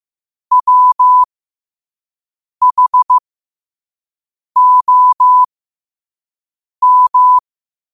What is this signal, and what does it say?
Morse code, heard as a keyed tone, about 15 wpm, spaced as for 5 wpm, 1000 Hz. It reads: WHOM
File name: 5wpm